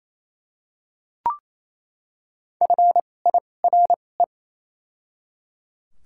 Morse code, heard as a keyed tone, about 28 wpm, spaced as for 20 wpm, 700 Hz. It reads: FIRE
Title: 2x Character Spacing - Top 500 Words - 28wpm